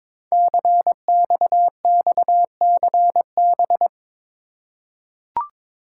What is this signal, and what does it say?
Morse code, heard as a keyed tone, about 22 wpm, 700 Hz. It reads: CXXCB
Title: Code Groups - 5 Letters - 22wpm